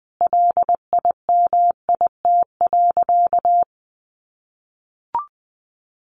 Morse code, heard as a keyed tone, about 19 wpm, 700 Hz. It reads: LIMIT.
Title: Sentences from Top 1500 Words 20wpm